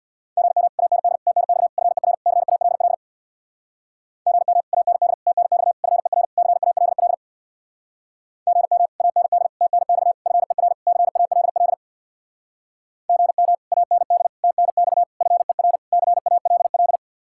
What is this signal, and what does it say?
Morse code, heard as a keyed tone, about 63 wpm, 700 Hz. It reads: GM AND TNX FER CALL GM AND TNX FER CALL GM AND TNX FER CALL GM AND TNX FER CALL